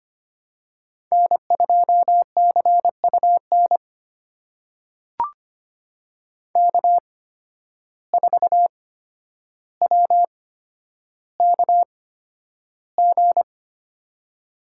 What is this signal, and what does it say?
Morse code, heard as a keyed tone, about 25 wpm, spaced as for 11 wpm, 700 Hz. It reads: N2CUN K4WKG
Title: USA Call Signs - 5 Characters - 8x Character Spacing - 25wpm